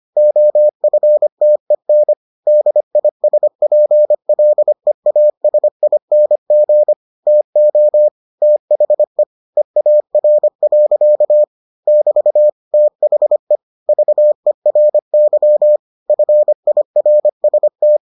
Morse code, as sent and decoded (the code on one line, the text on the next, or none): --- ..-. - . -. / -.. .. ... .--. .-.. .- ... .. -. --. / - --- / - .... / . .- .-. .-.-.- / -...- / - .... / ...- . .-. -.-- / ..-. .. .-. ... -
OFTEN DISPLEASING TO THE EAR. <BT> THE VERY FIRST